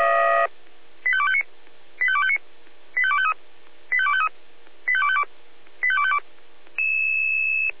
Five-tone selective call (ZVEI), heard as quick runs of five tones, tones of about 70 ms. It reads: 85259 85259 85253 85253 85252 85252